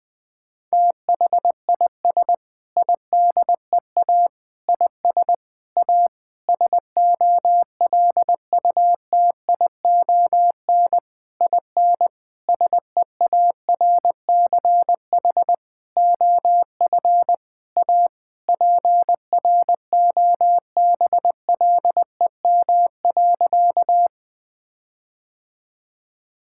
Morse code, heard as a keyed tone, about 21 wpm, 700 Hz. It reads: THIS IDEA IS A SOLUTION IN SEARCH OF A PROBLEM.
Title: Sentences with 8-Word Phrases - 20wpm